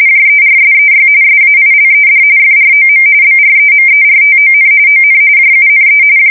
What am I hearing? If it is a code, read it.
RTTY, about 45.45 baud. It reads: JERSEY SAILED FROM BREMERTON ON JUNE
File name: fsk170 / battleshipnj